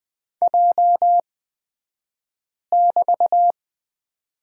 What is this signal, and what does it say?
Morse code, heard as a keyed tone, about 20 wpm, spaced as for 5 wpm, 700 Hz. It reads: J<BT>
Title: 235 Lesson 38d - Letter q - Two Characters 20wpm